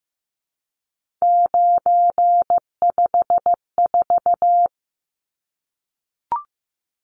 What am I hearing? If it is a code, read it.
Morse code, heard as a keyed tone, about 15 wpm, 700 Hz. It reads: 954
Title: Numbers - 3-Digit 15wpm